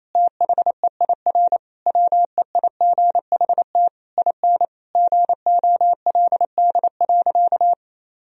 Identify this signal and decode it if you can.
Morse code, heard as a keyed tone, about 27 wpm, 700 Hz. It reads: THEIR WEIGHT IN GOLD.
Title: Sentences from Top 1000 Words - Review Part 3 - 28wpm